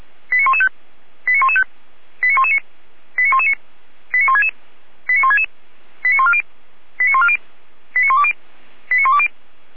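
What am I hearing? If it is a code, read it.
Five-tone selective call (ZVEI), heard as quick runs of five tones, tones of about 70 ms. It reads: 89115 89115 89119 89119 89166 89166 89140 89140 89120 89120
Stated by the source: ZVEI five-tone call received by SDR.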